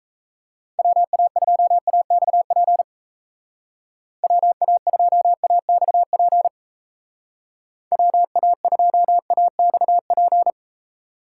Morse code, heard as a keyed tone, about 34 wpm, 700 Hz. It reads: WA2AXP WA2AXP WA2AXP
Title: Speed Racing - USA Call Signs 28wpm